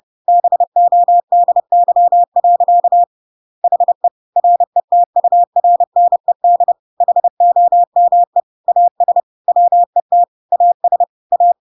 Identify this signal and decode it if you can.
Morse code, heard as a keyed tone, about 30 wpm, 700 Hz. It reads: BODY. HE RETURNED HOME AS WET AS A
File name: Pinocchio / Pinocchio ch6 30 30 ik1hge